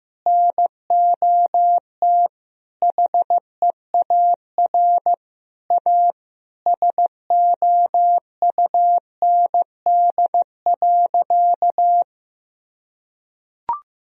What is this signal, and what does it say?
Morse code, heard as a keyed tone, about 15 wpm, 700 Hz. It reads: NOT HEAR A SOUND.
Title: Sentences from Top 1000 Words - Review Part 1 - 15wpm